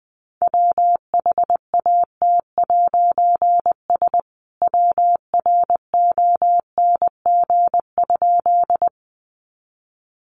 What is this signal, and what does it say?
Morse code, heard as a keyed tone, about 20 wpm, 700 Hz. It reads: WHAT'S WRONG?
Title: A Sound Of Thunder 20wpm